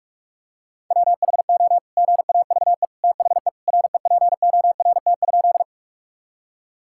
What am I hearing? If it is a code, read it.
Morse code, heard as a keyed tone, about 46 wpm, 700 Hz. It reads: WHO GAVE THE REPORT?
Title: Sentences from Top 1000 Words - Review Part 6 - 45wpm